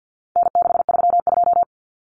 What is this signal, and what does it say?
Morse code, heard as a keyed tone, about 50 wpm, 700 Hz. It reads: N632
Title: Common Cut Numbers - 4-Digit 50wpm